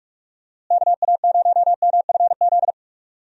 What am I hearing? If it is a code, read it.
Morse code, heard as a keyed tone, about 46 wpm, 700 Hz. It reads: KA0MFZ
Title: USA Call Signs - 6 Characters - Repeated 3x - 45wpm